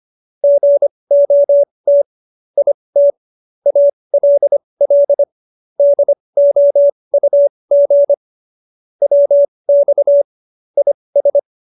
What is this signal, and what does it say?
Morse code, heard as a keyed tone, about 25 wpm, spaced as for 20 wpm, 570 Hz. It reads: GOT IT ALL DOUG WX IS
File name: QSO303 20